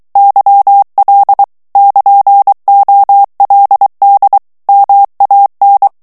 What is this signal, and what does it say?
Morse code, heard as a keyed tone, about 23 wpm, 800 Hz. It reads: YL <KN>OLD MAN
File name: morse mp3 / ITALIEN texte clair 1 vitesse 22 mots minute